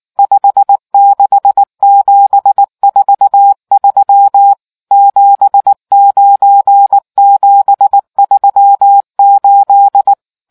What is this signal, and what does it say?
Morse code, heard as a keyed tone, about 19 wpm, 800 Hz. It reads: 56743 79738